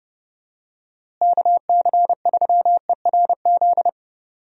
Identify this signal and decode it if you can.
Morse code, heard as a keyed tone, about 31 wpm, 700 Hz. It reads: KC3ERZ
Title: USA Call Signs - 6 Characters - Repeated 3x - 30wpm